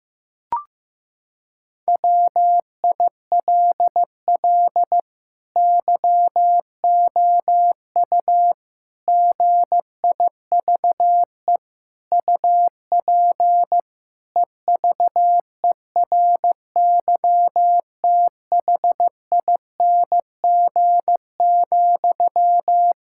Morse code, heard as a keyed tone, about 15 wpm, 700 Hz. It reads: WILL YOU GIVE UP EVERYTHING,
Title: Sentences from Top 1000 Words - Review Part 5 - 15wpm